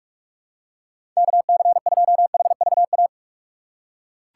Morse code, heard as a keyed tone, about 46 wpm, 700 Hz. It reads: KC2HVA
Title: USA Call Signs - 6 Characters - 4x Character Spacing - 45wpm